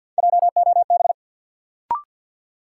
Morse code, heard as a keyed tone, about 52 wpm, 700 Hz. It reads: JOB
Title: Sentences from Top 600 Words - New Word Drill - 50wpm